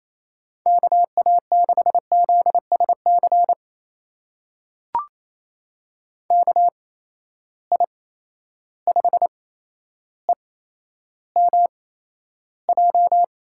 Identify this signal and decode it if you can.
Morse code, heard as a keyed tone, about 28 wpm, spaced as for 12 wpm, 700 Hz. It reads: KA6ZSC KI5EMJ
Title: USA Call Signs - 6 Characters - 8x Character Spacing - 28wpm